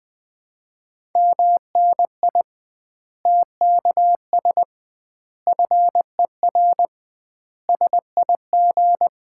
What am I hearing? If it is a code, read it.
Morse code, heard as a keyed tone, about 20 wpm, 700 Hz. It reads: MNI TKS FER SIG
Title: Common QSO Phrases - 2x Word-Spacing - 20wpm